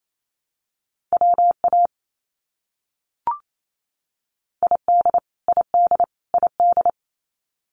Morse code, heard as a keyed tone, about 28 wpm, 700 Hz. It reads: WA ID ID ID
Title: US States Repeated 3x 28wpm